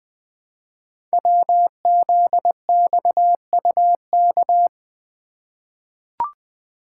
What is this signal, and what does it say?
Morse code, heard as a keyed tone, about 20 wpm, 700 Hz. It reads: WZXUK
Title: Code Groups - 5 Letter:Numbers - 20wpm